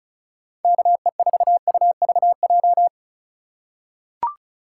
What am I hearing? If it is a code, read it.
Morse code, heard as a keyed tone, about 36 wpm, 700 Hz. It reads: KE4UVJ
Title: USA Call Signs - 6 Characters - 4x Character Spacing - 35wpm